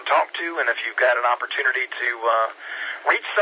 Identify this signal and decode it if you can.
Emergency Alert System (SAME) tones, heard as harsh buzzing bursts, with speech. It is heard from 0.5 to 3.0 s.